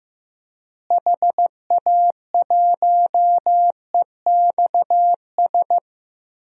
Morse code, heard as a keyed tone, about 15 wpm, 700 Hz. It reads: HA1EXS
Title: DXCC-100 Call Signs - 6 Characters - Repeated 3x - 15wpm